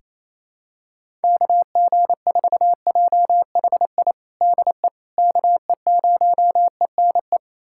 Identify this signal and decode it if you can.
Morse code, heard as a keyed tone, about 28 wpm, 700 Hz. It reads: KG4JHI DE KE0ENE
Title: USA Callsign DE Callsign 28wpm